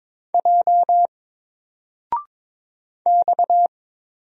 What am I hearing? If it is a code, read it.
Morse code, heard as a keyed tone, about 22 wpm, 700 Hz. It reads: J X
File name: Single-Letter-Number-22wpm